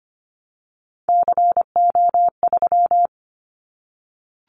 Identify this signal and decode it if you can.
Morse code, heard as a keyed tone, about 25 wpm, 700 Hz. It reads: CO3
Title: Callsign Prefixes for Top 100 DXCC - 25wpm